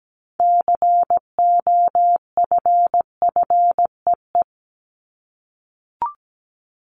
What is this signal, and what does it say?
Morse code, heard as a keyed tone, about 17 wpm, 700 Hz. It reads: COFFEE
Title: Top 1751-2000 Words 17wpm